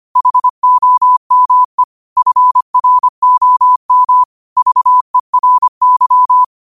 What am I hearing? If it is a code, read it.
Morse code, heard as a keyed tone, about 25 wpm, 1000 Hz. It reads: HOME FROM VERY